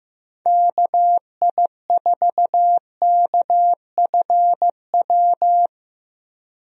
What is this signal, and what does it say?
Morse code, heard as a keyed tone, about 16 wpm, 700 Hz. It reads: KI4KFW
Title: USA Call Signs - 6 Characters - Repeated 3x - 15wpm